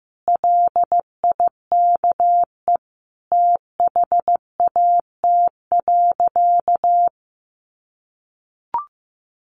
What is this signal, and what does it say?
Morse code, heard as a keyed tone, about 15 wpm, 700 Hz. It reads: LIKE THAT.
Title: Sentences from Top 300 Words 15wpm